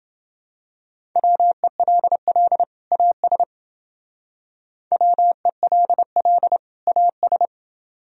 Morse code, heard as a keyed tone, about 28 wpm, 700 Hz. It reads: WELL AS WELL AS